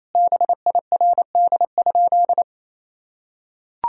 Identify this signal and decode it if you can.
Morse code, heard as a keyed tone, about 27 wpm, 700 Hz. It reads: BIRD?
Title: Sentences from Top 1000 Words - Review Part 2 - 28wpm